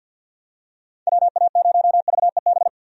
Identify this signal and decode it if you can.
Morse code, heard as a keyed tone, about 51 wpm, 700 Hz. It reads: WA0VEB
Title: USA Call Signs - 6 Characters - 50wpm